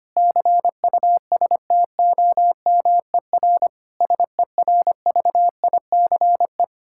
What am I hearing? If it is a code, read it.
Morse code, heard as a keyed tone, about 25 wpm, 700 Hz. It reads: CUSTOMER SERVICE
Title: Practice QSO - Exchange 3 - More Info - 25wpm